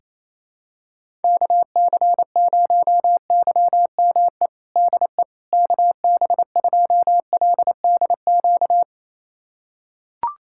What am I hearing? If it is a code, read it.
Morse code, heard as a keyed tone, about 28 wpm, 700 Hz. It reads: KC0YME DE KB2LDQ
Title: USA Callsign DE Callsign - 6 Characters - 28wpm